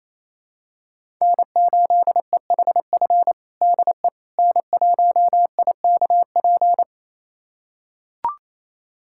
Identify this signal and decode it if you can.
Morse code, heard as a keyed tone, about 28 wpm, 700 Hz. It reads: N8EHF DE N1IKP
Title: USA Callsign DE Callsign - 5 Characters - 28wpm